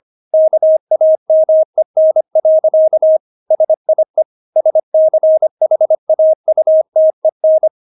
Morse code, heard as a keyed tone, about 25 wpm, 630 Hz. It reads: KAMEN. SIE SCHAUTEN
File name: Witz6 25wpm